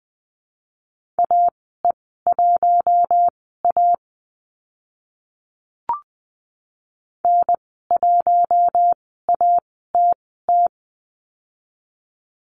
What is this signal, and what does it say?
Morse code, heard as a keyed tone, about 20 wpm, spaced as for 14 wpm, 700 Hz. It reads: AE1A N1ATT
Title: 042 Lesson 8g - Number 1 - Call Signs 20wpm